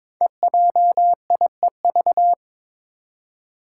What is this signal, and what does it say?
Morse code, heard as a keyed tone, about 22 wpm, 700 Hz. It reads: EJIEV
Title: Five Challenging Letter-Number Combinations 22wpm